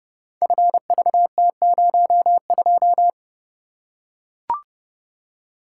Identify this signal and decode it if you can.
Morse code, heard as a keyed tone, about 30 wpm, 700 Hz. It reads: FVT02
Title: Code Groups - 5 Letter:Numbers - 30wpm